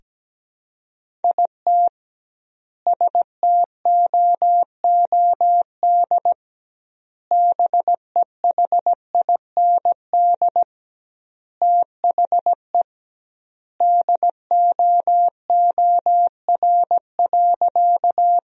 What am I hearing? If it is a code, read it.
Morse code, heard as a keyed tone, about 17 wpm, 700 Hz. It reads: IT STOOD BEHIND THE DOOR.
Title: 2x Word-Spacing - Sentences from Top 500 Words - 17wpm